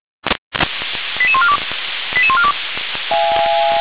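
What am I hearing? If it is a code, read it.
Five-tone selective call (ZVEI), heard as quick runs of five tones, tones of about 70 ms. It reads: 88142 88142
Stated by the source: ZVEI five-tone selective call 88142 received by SDR.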